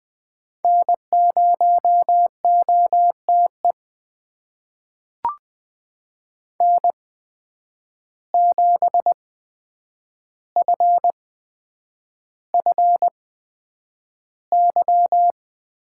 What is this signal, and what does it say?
Morse code, heard as a keyed tone, about 20 wpm, spaced as for 9 wpm, 700 Hz. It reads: N0OTE N7FFY
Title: USA Call Signs - 8x Character Spacing - 20wpm